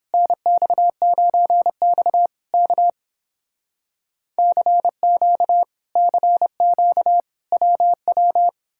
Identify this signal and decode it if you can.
Morse code, heard as a keyed tone, about 31 wpm, 700 Hz. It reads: NX9X K CQ CQ WW